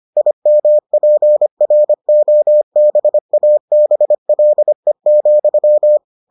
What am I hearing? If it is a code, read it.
Morse code, heard as a keyed tone, about 25 wpm, 600 Hz. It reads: IMPROBABLE,